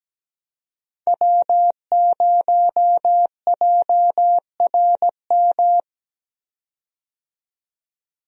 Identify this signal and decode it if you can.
Morse code, heard as a keyed tone, about 17 wpm, 700 Hz. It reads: W0JRM
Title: USA Call Signs - 5 Characters - 17wpm